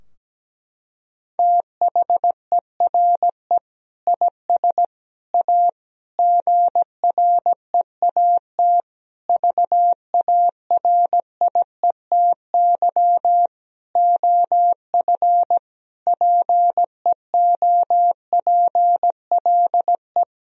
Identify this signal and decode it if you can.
Morse code, heard as a keyed tone, about 17 wpm, 700 Hz. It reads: THERE IS A GREAT VARIETY OF PEOPLE
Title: Sentences from Top 1000 Words - Review Part 4 - 17wpm